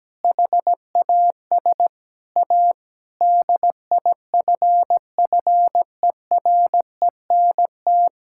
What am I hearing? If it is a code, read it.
Morse code, heard as a keyed tone, about 17 wpm, 700 Hz. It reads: HAS A DIFFERENT
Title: Sentences from Top 1000 Words - Review Part 6 - 17wpm